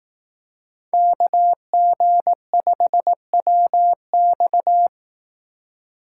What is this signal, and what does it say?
Morse code, heard as a keyed tone, about 19 wpm, 700 Hz. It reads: KG5WX
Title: Speed Racing - USA 2x2 Call Signs 15wpm